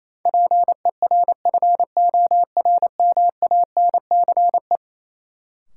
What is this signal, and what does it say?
Morse code, heard as a keyed tone, about 28 wpm, 700 Hz. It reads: PERFORMANCE
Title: Sentences from Top 1000 Words - New Word Drill - 28wpm